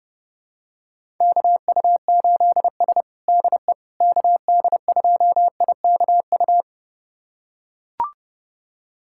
Morse code, heard as a keyed tone, about 30 wpm, 700 Hz. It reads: KU8S DE KD2IKU
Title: USA Callsign DE Callsign 30wpm